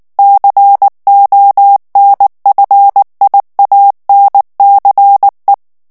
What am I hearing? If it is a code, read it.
Morse code, heard as a keyed tone, about 19 wpm, 800 Hz. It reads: CONFIANCE
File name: CW / FICHIERS-MP3 / FRANCAIS TEXTE CLAIR 2 VITESSE 18 MOTS MINUTES